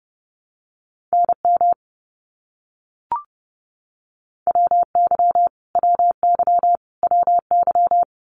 Morse code, heard as a keyed tone, about 30 wpm, 700 Hz. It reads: NM WY WY WY
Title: US States Repeated 3x 30wpm